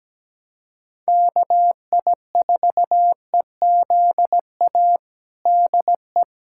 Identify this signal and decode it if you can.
Morse code, heard as a keyed tone, about 17 wpm, 700 Hz. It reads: KI4EZA DE
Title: USA Callsign DE Callsign - 6 Characters - 17wpm